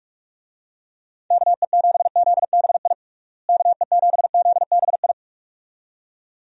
Morse code, heard as a keyed tone, about 48 wpm, 700 Hz. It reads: KE7ZBI KE7ZBI
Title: USA Call Signs - 6 Characters - Repeated 2x - 45wpm